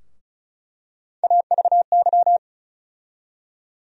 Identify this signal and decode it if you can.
Morse code, heard as a keyed tone, about 34 wpm, 700 Hz. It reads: AVY
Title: USA Call Sign Postfixes - 3 Letters 35wpm